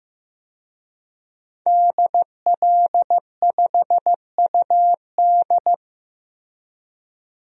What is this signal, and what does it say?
Morse code, heard as a keyed tone, about 15 wpm, 700 Hz. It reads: DL5UD
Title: DXCC-100 Call Signs - 5 Characters - Repeated 2x - 15wpm